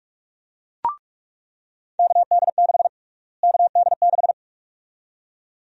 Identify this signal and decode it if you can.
Morse code, heard as a keyed tone, about 41 wpm, 700 Hz. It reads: KD6 KD6
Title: USA Call Sign Prefixes - Repeated 2x - 45wpm